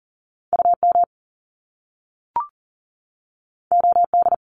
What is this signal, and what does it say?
Morse code, heard as a keyed tone, about 40 wpm, 700 Hz. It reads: UM OD
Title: Top Two Letter Combinations in Words 40wpm